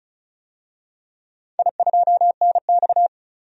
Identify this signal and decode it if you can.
Morse code, heard as a keyed tone, about 36 wpm, 700 Hz. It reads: I2NX
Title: DXCC-100 Call Signs - 4 Characters - 35wpm